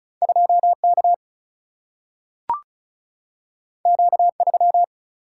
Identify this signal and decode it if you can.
Morse code, heard as a keyed tone, about 36 wpm, 700 Hz. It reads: 2 K Q 3
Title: Single-Letter Single-Number 35wpm